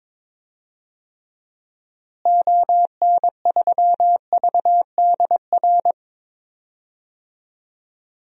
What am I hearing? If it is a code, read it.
Morse code, heard as a keyed tone, about 22 wpm, 700 Hz. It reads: ON3VDR
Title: DXCC-100 Call Signs - 6 Characters - Repeated 2x - 22wpm